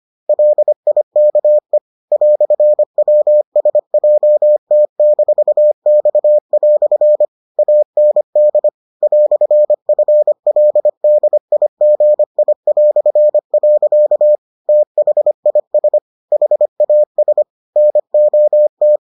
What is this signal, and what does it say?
Morse code, heard as a keyed tone, about 25 wpm, 600 Hz. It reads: LIKE "WSJT-X" AND "FLDIGI". THIS HAS NOT